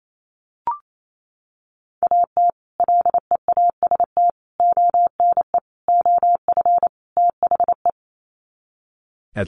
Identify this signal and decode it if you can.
Morse code, heard as a keyed tone, about 28 wpm, 700 Hz. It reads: AT LEAST ONE OF THE
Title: Sets of 5 Words Encore 28wpm